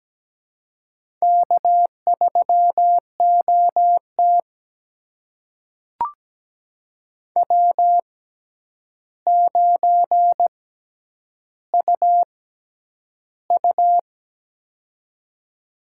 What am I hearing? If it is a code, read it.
Morse code, heard as a keyed tone, about 17 wpm, spaced as for 9 wpm, 700 Hz. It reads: K3OT W9UU
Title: USA Call Signs - 4 Characters - 6x Character Spacing - 17wpm